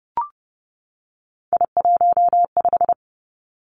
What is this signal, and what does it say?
Morse code, heard as a keyed tone, about 30 wpm, 700 Hz. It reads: I15
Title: Three Challenging Letter-Number Combinations 30wpm